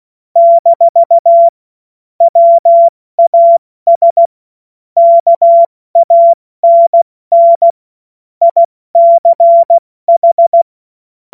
Morse code, heard as a keyed tone, about 16 wpm, 680 Hz. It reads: - WAS KANN ICH
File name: Witz-07-Oktober-2025-70-F80